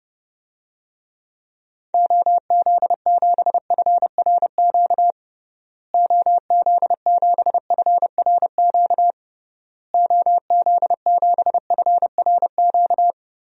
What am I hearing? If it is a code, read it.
Morse code, heard as a keyed tone, about 30 wpm, 700 Hz. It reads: OZ7FRQ OZ7FRQ OZ7FRQ
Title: DXCC-100 Call Signs - 6 Characters - Repeated 3x - 30wpm